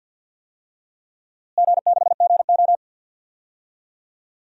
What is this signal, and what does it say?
Morse code, heard as a keyed tone, about 48 wpm, 700 Hz. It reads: G6GO
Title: DXCC-100 Call Signs - 4 Characters - Repeated 3x - 50wpm